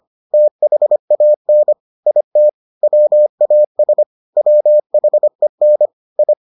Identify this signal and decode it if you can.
Morse code, heard as a keyed tone, about 25 wpm, 600 Hz. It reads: THAN IT WAS WHEN I